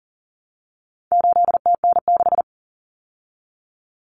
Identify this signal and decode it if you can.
Morse code, heard as a keyed tone, about 38 wpm, 700 Hz. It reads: 8TN6
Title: Common Cut Numbers - 4-Digit 40wpm